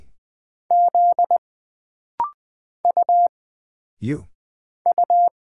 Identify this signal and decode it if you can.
Morse code, heard as a keyed tone, about 20 wpm, 700 Hz. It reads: Z U U
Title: Single Character (Letter:Number:Punctuation:Prosign) - Rapid-Fire - 20wpm